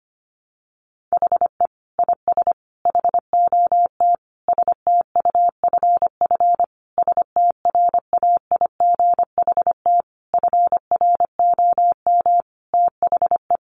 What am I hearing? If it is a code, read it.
Morse code, heard as a keyed tone, about 25 wpm, 700 Hz. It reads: HE IS HOT STUFF STRAIGHT FROM THE
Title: Sentences from Top 1500 Words 25wpm